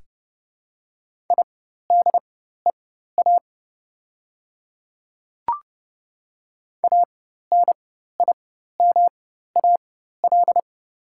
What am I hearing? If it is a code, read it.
Morse code, heard as a keyed tone, about 30 wpm, spaced as for 14 wpm, 700 Hz. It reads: IDEA ANIMAL
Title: 4x Character Spacing - Top 300 Words - 30wpm